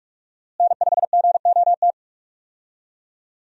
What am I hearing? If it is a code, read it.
Morse code, heard as a keyed tone, about 45 wpm, 700 Hz. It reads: N5GOT
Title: USA Call Signs - 5 Characters - Repeated 2x - 45wpm